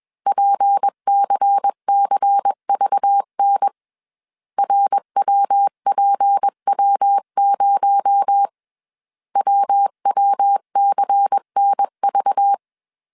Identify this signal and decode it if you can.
Morse code, heard as a keyed tone, about 21 wpm, 800 Hz. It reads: PCCVN RWPW0 WWCNV
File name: CW / DET / 1 6 1500C S C 20wpm 2116